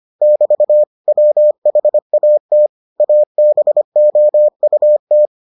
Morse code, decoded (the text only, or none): <BT> WHAT ABOUT